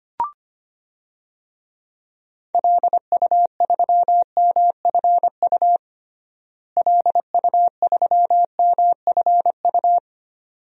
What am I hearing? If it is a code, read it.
Morse code, heard as a keyed tone, about 25 wpm, 700 Hz. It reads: LU3MFU LU3MFU
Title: DXCC-100 Call Signs - 6 Characters - Repeated 2x - 25wpm